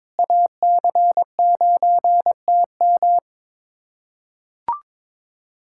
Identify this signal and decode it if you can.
Morse code, heard as a keyed tone, about 21 wpm, 700 Hz. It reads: AC9TM
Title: USA Callsign DE Callsign - 5 Characters - 22wpm